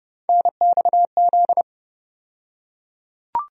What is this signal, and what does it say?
Morse code, heard as a keyed tone, about 29 wpm, 700 Hz. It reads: NXZ
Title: USA Call Sign Postfixes - 3 Letters 30wpm